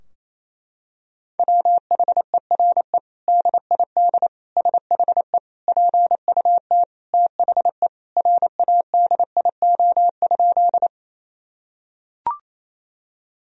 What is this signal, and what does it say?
Morse code, heard as a keyed tone, about 28 wpm, 700 Hz. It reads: WHERE DID SHE PUT THE RADIO?
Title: Sentences from Top 1000 Words - Review Part 3 - 28wpm